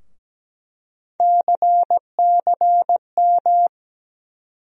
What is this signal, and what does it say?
Morse code, heard as a keyed tone, about 17 wpm, 700 Hz. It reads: CCM
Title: USA Call Sign Postfixes - 3 Letters 17wpm